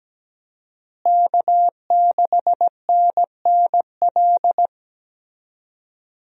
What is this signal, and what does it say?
Morse code, heard as a keyed tone, about 17 wpm, 700 Hz. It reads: K6NNL
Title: USA Call Signs - 5 Characters - Repeated 3x - 17wpm